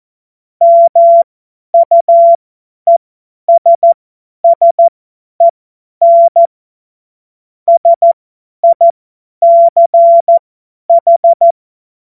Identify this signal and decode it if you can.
Morse code, heard as a keyed tone, about 14 wpm, spaced as for 10 wpm, 680 Hz. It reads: MUESSEN SICH
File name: Morsekurs / Witze / Witz 26. April 2022 50 F70